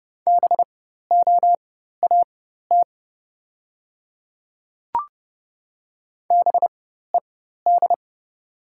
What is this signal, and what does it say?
Morse code, heard as a keyed tone, about 30 wpm, spaced as for 14 wpm, 700 Hz. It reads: BOAT BED
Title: 4x Character Spacing - Top 500 Words - 30wpm